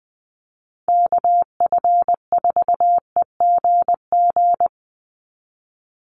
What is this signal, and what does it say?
Morse code, heard as a keyed tone, about 20 wpm, 700 Hz. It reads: KF4EGG
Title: USA Call Signs 20wpm